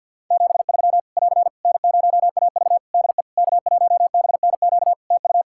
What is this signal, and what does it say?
Morse code, heard as a keyed tone, about 50 wpm, 700 Hz. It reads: 73 <AR> N0AV DE K1BNQ TU
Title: Practice QSO - Exchange 4 - Ending - 50wpm